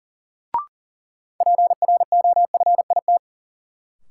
Morse code, heard as a keyed tone, about 40 wpm, 700 Hz. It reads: PROFIT
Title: Sentences from Top 800 Words - New Word Drill - 40wpm